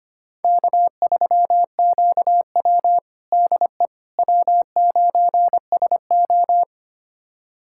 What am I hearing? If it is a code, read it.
Morse code, heard as a keyed tone, about 25 wpm, 700 Hz. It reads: K3QW DE W9SO
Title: USA Callsign DE Callsign - 4 Characters - 25wpm